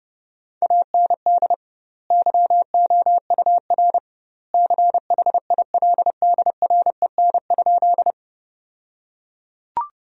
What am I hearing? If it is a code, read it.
Morse code, heard as a keyed tone, about 30 wpm, 700 Hz. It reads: AND YOUR CHILDREN?
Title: 2x Word-Spacing - Sentences from Top 300 Words - 30wpm